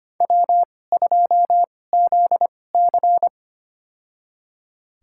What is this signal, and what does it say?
Morse code, heard as a keyed tone, about 25 wpm, spaced as for 18 wpm, 700 Hz. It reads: W2ZC
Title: USA Call Signs - 4 Characters - 2x Character Spacing - 25wpm